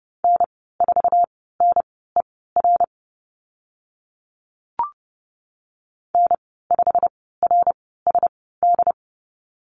Morse code, heard as a keyed tone, about 32 wpm, spaced as for 17 wpm, 690 Hz. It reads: N4NER N5RSD
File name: Lesson 15g - Number 5 - Call Signs 30wpm